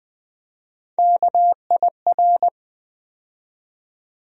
Morse code, heard as a keyed tone, about 20 wpm, 700 Hz. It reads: KIR
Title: USA Call Sign Postfixes - 3 Letters 20wpm